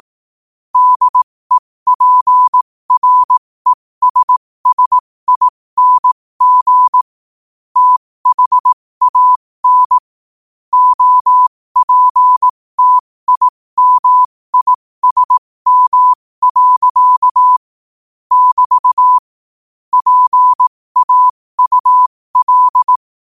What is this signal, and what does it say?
Morse code, heard as a keyed tone, about 18 wpm, spaced as for 15 wpm, 1000 Hz. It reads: DEPRESSING THAN OPTIMISM. <BT> PAUL
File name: qotdfiles / 15wpm